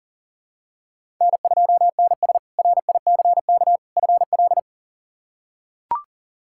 Morse code, heard as a keyed tone, about 37 wpm, 700 Hz. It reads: N2NS RICK FL
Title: K1USN SST QSO Practice - 40wpm